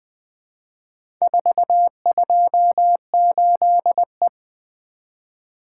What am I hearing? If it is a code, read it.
Morse code, heard as a keyed tone, about 20 wpm, 700 Hz. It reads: 428E
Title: Four Letter-Number Combinations - 20wpm